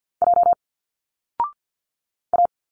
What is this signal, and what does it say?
Morse code, heard as a keyed tone, about 50 wpm, 700 Hz. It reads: J A
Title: Single Letters 50wpm